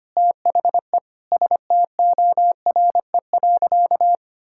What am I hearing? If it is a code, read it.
Morse code, heard as a keyed tone, about 25 wpm, 700 Hz. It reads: THE STORE.
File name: practice / 25wpm / Sentences with 9-Word Phrases - 25wpm